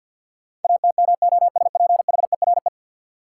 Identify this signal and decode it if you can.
Morse code, heard as a keyed tone, about 52 wpm, 700 Hz. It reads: ATMOSPHERE
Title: Sentences from Top 1600 Words - New Word Drill - 50wpm